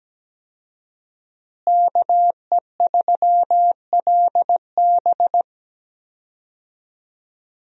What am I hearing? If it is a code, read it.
Morse code, heard as a keyed tone, about 17 wpm, 700 Hz. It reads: KE3LB
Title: USA Call Signs - 5 Characters - Repeated 3x - 17wpm